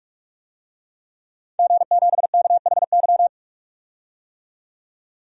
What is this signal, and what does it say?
Morse code, heard as a keyed tone, about 41 wpm, 700 Hz. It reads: G7KHY
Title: DXCC-100 Call Signs - 5 Characters - 45wpm